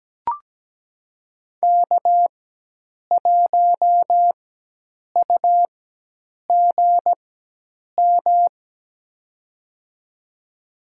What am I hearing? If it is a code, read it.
Morse code, heard as a keyed tone, about 17 wpm, spaced as for 8 wpm, 700 Hz. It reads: K1UGM
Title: USA Call Signs - 5 Characters - 4x Character Spacing - 17wpm